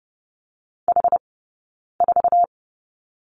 Morse code, heard as a keyed tone, about 30 wpm, 700 Hz. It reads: H 4
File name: Lesson 20d - Number 3 - Two Characters 30wpm